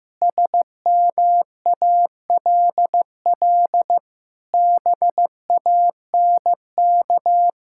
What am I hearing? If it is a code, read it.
Morse code, heard as a keyed tone, about 15 wpm, 700 Hz. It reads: SMALL BANK
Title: Sentences from Top 1000 Words - Review Part 6 - 15wpm